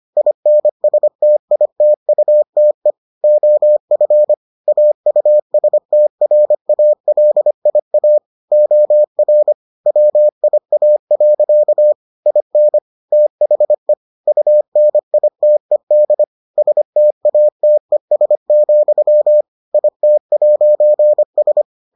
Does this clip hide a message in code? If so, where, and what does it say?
Yes: Morse code from 0.2 to 21.6 s, heard as a keyed tone, about 25 wpm, 600 Hz. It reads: INSTITUTE OF AUSTRALIA OR WIA. IN THE UNITED STATES, IT'S